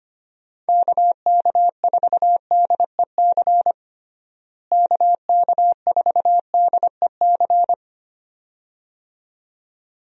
Morse code, heard as a keyed tone, about 25 wpm, 700 Hz. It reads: KK4DEC KK4DEC
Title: USA Call Signs - 6 Characters - Repeated 2x - 25wpm